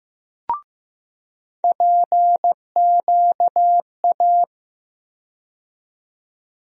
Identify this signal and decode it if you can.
Morse code, heard as a keyed tone, about 15 wpm, 700 Hz. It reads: PQA
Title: USA Call Sign Postfixes - 3 Letters 15wpm